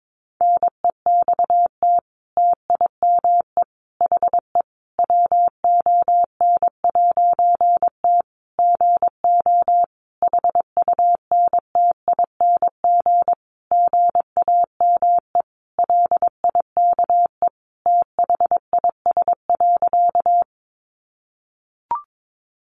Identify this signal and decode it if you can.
Morse code, heard as a keyed tone, about 22 wpm, 700 Hz. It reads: NEXT TIME HE WON'T GO HUNTING GAME LIKE THIS.
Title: A Sound Of Thunder 22wpm